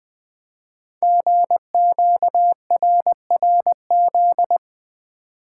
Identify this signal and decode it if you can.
Morse code, heard as a keyed tone, about 20 wpm, 700 Hz. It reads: GQRRZ
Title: Code Groups - 5 Letters - 20wpm